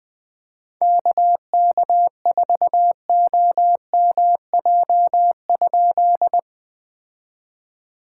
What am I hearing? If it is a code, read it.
Morse code, heard as a keyed tone, about 20 wpm, 700 Hz. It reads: KK4OMJ?
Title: POTA QSO Practice - 20wpm